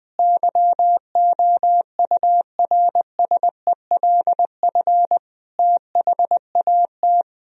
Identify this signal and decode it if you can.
Morse code, heard as a keyed tone, about 21 wpm, 700 Hz. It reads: YOURSELF THAT
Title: Sentences from Top 1000 Words - Review Part 1 - 20wpm